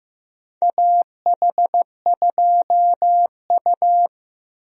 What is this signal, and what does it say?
Morse code, heard as a keyed tone, about 15 wpm, 700 Hz. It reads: AH2U
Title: USA Call Signs - 4 Characters - Repeated 3x - 15wpm